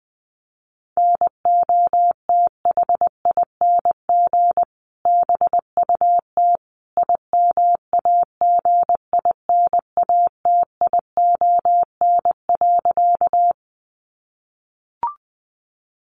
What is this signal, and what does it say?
Morse code, heard as a keyed tone, about 19 wpm, 700 Hz. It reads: NOTHING BUT IMAGINATION.
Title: Sentences from Top 1300 Words 20wpm